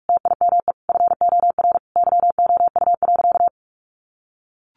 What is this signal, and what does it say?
Morse code, heard as a keyed tone, about 44 wpm, 700 Hz. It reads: TIME FOR YOU.
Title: Sentences from Top 1100 Words 45wpm